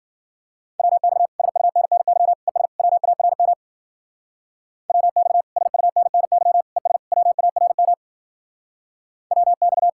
Morse code, heard as a keyed tone, about 50 wpm, 700 Hz. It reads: WX SUNNY ES WARM WX SUNNY ES WARM WX
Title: Speed-Racing Common QSO Phrases Encore - 40wpm